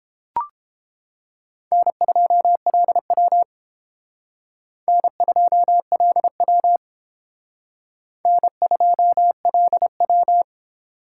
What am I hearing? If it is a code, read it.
Morse code, heard as a keyed tone, about 29 wpm, 700 Hz. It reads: N2LW N2LW N2LW